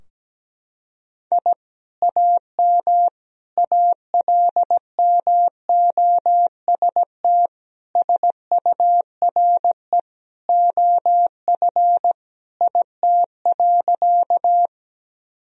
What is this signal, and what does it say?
Morse code, heard as a keyed tone, about 17 wpm, 700 Hz. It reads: I AM ALMOST SURE OF IT.
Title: Sentences from Top 1000 Words - Review Part 4 - 17wpm